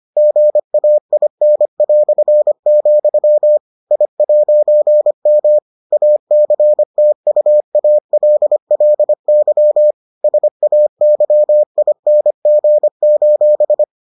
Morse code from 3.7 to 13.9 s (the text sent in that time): I'M ACTUALLY SAYING: